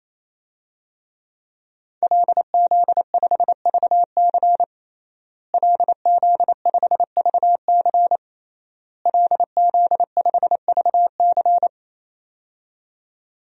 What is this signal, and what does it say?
Morse code, heard as a keyed tone, about 28 wpm, 700 Hz. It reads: LZ5VC LZ5VC LZ5VC
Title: DXCC-100 Call Signs - 5 Characters - Repeated 3x - 28wpm